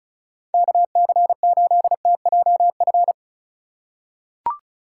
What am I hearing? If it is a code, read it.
Morse code, heard as a keyed tone, about 36 wpm, 700 Hz. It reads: KC8TJF
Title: USA Call Signs - 6 Characters - 3x Character Spacing - 35wpm